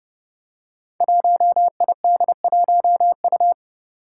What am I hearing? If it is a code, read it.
Morse code, heard as a keyed tone, about 30 wpm, 700 Hz. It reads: 1ID1U
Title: Five Challenging Letter-Number Combinations 30wpm